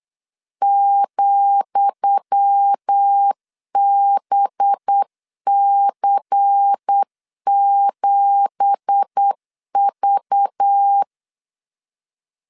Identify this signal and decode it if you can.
Morse code, heard as a keyed tone, about 8 wpm, 800 Hz. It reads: ,BC7V
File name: LEZIONI / 10 SIGNS / 10D COMA 8wpm 3749